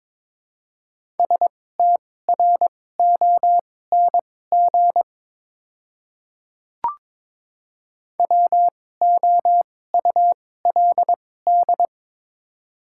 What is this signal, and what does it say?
Morse code, heard as a keyed tone, about 22 wpm, spaced as for 16 wpm, 700 Hz. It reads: STRONG WOULD
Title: 2x Character Spacing - Top 500 Words - 22wpm